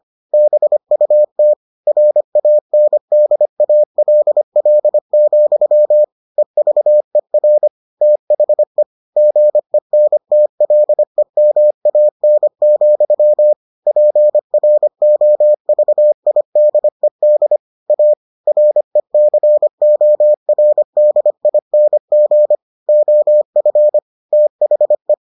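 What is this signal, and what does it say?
Morse code, heard as a keyed tone, about 25 wpm, 600 Hz. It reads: BUT RANDALL, EVER THE GENTLEMAN, PROVIDED A RECORDING OF THE
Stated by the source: Morse edition of the Foundations of Amateur Radio episodes.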